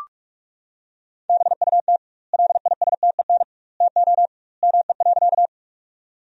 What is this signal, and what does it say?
Morse code, heard as a keyed tone, about 46 wpm, 700 Hz. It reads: BUT LISTEN TO ME.
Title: 2x Word-Spacing - Sentences from Top 400 Words - 45wpm